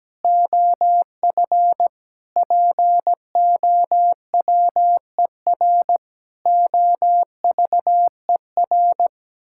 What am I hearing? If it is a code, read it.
Morse code, heard as a keyed tone, about 17 wpm, 700 Hz. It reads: OF POWER OVER
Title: Sentences from Top 1000 Words - Review Part 5 - 17wpm